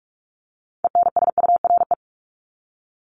Morse code, heard as a keyed tone, about 44 wpm, 700 Hz. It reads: ENSURE
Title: Top 1001-1250 Words 45wpm